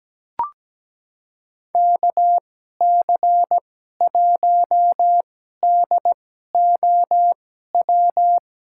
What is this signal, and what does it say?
Morse code, heard as a keyed tone, about 17 wpm, spaced as for 12 wpm, 700 Hz. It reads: KC1DOW
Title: USA Call Signs - 6 Characters - 2x Character Spacing - 17wpm